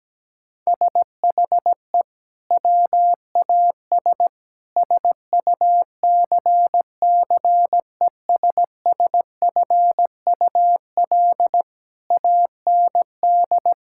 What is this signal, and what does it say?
Morse code, heard as a keyed tone, about 17 wpm, 700 Hz. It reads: SHE WAS SUCCESSFUL AND
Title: Sentences from Top 1000 Words - Review Part 6 - 17wpm